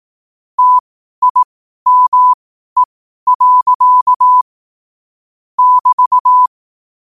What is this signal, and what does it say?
Morse code, heard as a keyed tone, about 18 wpm, spaced as for 12 wpm, 1000 Hz. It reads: TIME. <BT>
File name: qotdfiles / 12wpm